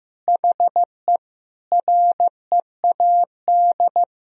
Morse code, heard as a keyed tone, about 15 wpm, 700 Hz. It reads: HE READ
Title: Sentences from Top 1000 Words - Review Part 3 - 15wpm